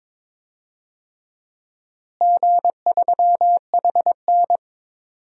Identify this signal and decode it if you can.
Morse code, heard as a keyed tone, about 22 wpm, 700 Hz. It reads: G3HN
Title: DXCC-100 Call Signs - 4 Characters - Repeated 3x - 22wpm